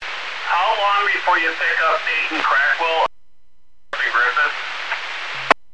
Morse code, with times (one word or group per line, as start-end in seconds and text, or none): none